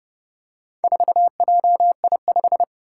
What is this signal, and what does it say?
Morse code, heard as a keyed tone, about 31 wpm, 700 Hz. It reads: 4JI5
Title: Four Challenging Letter-Number Combinations 30wpm